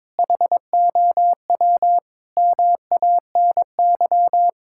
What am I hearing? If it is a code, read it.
Morse code, heard as a keyed tone, about 22 wpm, 700 Hz. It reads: HOW MANY
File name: practice / 22wpm / Sets of 6 Words - 22wpm